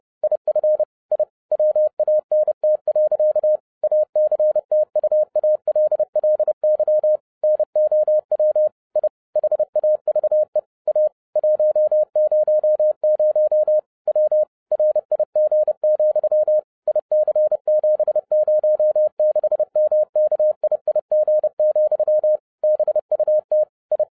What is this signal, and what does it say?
Morse code, heard as a keyed tone, about 30 wpm, 610 Hz. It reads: IF I WANT. ACTUALLY NOW I HAVE A 100 W RIG, IC706MKIIG, BUT I
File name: 2011-07-mp3 / n40 EXCITING QRP CW OPERATION 30wpm